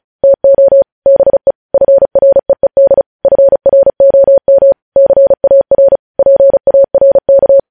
Morse code, heard as a keyed tone, about 36 wpm, 550 Hz. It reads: TO BE FREED FROM CAR PARK
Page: news headlines in morse code at 35 wpm